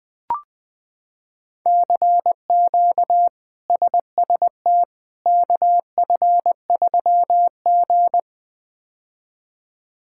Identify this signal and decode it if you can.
Morse code, heard as a keyed tone, about 20 wpm, 700 Hz. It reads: CQ SST KF3G
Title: K1USN SST QSO Practice - 20wpm